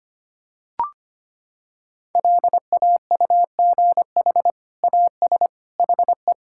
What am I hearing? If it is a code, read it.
Morse code, heard as a keyed tone, about 25 wpm, 700 Hz. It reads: LAUGH AS HE
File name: practice / 25wpm / Sentences with 2-Word Phrases - 25wpm